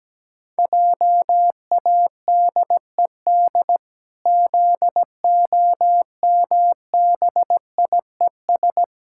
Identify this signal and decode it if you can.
Morse code, heard as a keyed tone, about 17 wpm, 700 Hz. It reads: JADED ZOMBIES
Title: Pangram Challenge v2 - 17wpm